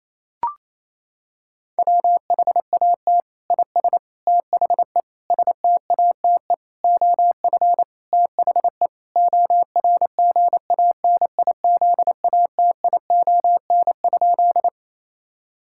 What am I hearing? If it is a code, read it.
Morse code, heard as a keyed tone, about 28 wpm, 700 Hz. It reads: WHAT IS THE STATE OF THE ORGANIZATION?
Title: Sentences from Top 1000 Words - Review Part 5 - 28wpm